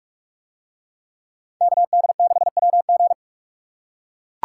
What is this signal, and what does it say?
Morse code, heard as a keyed tone, about 42 wpm, 700 Hz. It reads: KD6WG